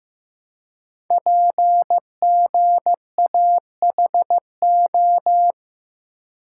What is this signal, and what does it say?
Morse code, heard as a keyed tone, about 15 wpm, 700 Hz. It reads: PGAHO
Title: Code Groups - 5 Letters - 15wpm